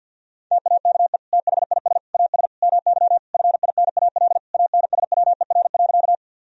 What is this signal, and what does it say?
Morse code, heard as a keyed tone, about 47 wpm, 700 Hz. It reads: TAKE THIS AS MY FINAL ANSWER.
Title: Sentences from Top 1000 Words - Review Part 2 - 50wpm